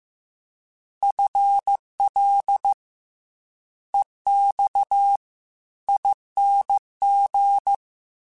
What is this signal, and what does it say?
Morse code, heard as a keyed tone, about 15 wpm, 800 Hz. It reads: FL EX ING
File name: FL / FL mp3s / Fast FL @13